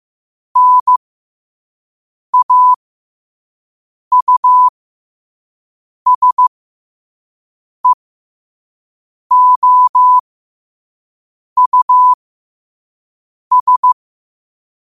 Morse code, heard as a keyed tone, about 15 wpm, spaced as for 5 wpm, 1000 Hz. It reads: NAUSEOUS
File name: qotdfiles / 5wpm